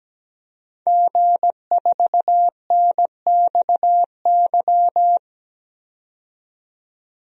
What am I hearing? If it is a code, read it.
Morse code, heard as a keyed tone, about 17 wpm, 700 Hz. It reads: G4NXY
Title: DXCC-100 Call Signs - 5 Characters - 17wpm